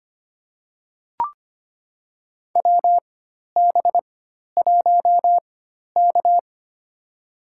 Morse code, heard as a keyed tone, about 25 wpm, spaced as for 12 wpm, 700 Hz. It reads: WB1K
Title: USA Call Signs - 4 Characters - 4x Character Spacing - 25wpm